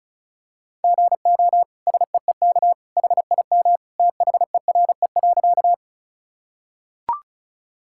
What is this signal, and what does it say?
Morse code, heard as a keyed tone, about 36 wpm, 700 Hz. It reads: GO SEEK HIM THERE.
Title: Sentences from Top 1600 Words 35wpm